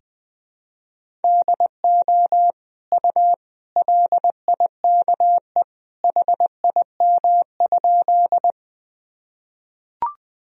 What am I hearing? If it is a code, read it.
Morse code, heard as a keyed tone, about 20 wpm, 700 Hz. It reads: DO U LIKE HIM?
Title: Sentences from Top 100 Words and Abbreviations 20wpm